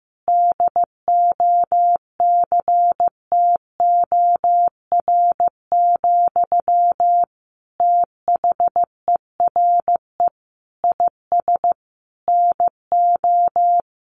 Morse code, heard as a keyed tone, about 14 wpm, 700 Hz. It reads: DOCTOR, THERE IS NO
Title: Sentences from Top 1500 Words 15wpm